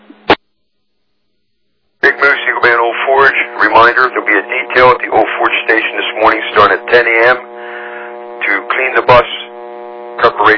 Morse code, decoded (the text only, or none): none